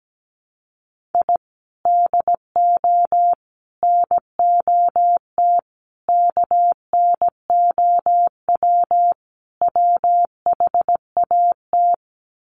Morse code, heard as a keyed tone, about 17 wpm, 700 Hz. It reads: I DO NOT KNOW WHAT
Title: Sets of 5 Words 17wpm